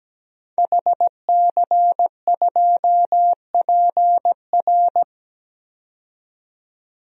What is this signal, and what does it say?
Morse code, heard as a keyed tone, about 17 wpm, 700 Hz. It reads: HC2PR
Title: DXCC-100 Call Signs - 5 Characters - 17wpm